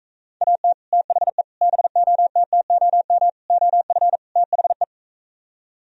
Morse code, heard as a keyed tone, about 45 wpm, 700 Hz. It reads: AT THE BOTTOM OF THE